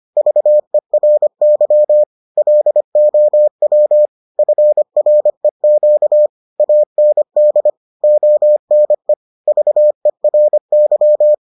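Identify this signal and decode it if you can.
Morse code, heard as a keyed tone, about 25 wpm, 600 Hz. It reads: VERY LOW FREQ AND ONE VERY
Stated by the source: Morse edition of the Foundations of Amateur Radio episodes.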